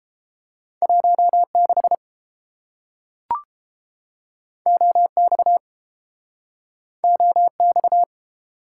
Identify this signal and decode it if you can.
Morse code, heard as a keyed tone, about 30 wpm, 700 Hz. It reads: 16 OX OX